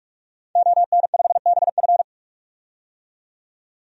Morse code, heard as a keyed tone, about 42 wpm, 700 Hz. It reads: ON5BF
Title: DXCC-100 Call Signs - 5 Characters - 45wpm